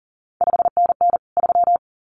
Morse code, heard as a keyed tone, about 39 wpm, 700 Hz. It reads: 5NN 3
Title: CQ WW DX Contest Exchange Encore - 40wpm